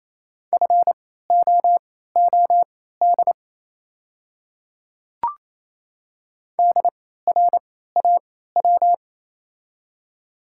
Morse code, heard as a keyed tone, about 28 wpm, spaced as for 16 wpm, 700 Hz. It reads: FOOD DRAW
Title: 3x Character Spacing - Top 300 Words - 28wpm